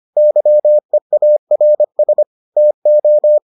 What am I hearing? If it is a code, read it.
Morse code, heard as a keyed tone, about 25 wpm, 600 Hz. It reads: YEARS TO